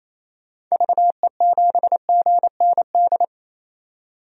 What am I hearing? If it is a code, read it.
Morse code, heard as a keyed tone, about 28 wpm, 700 Hz. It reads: VE7GND
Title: DXCC-100 Call Signs - 6 Characters - 28wpm